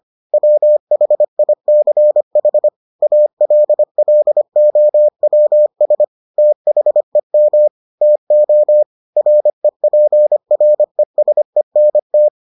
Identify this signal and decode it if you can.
Morse code, heard as a keyed tone, about 25 wpm, 600 Hz. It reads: WHICH ALLOWS THEM TO REPRESENT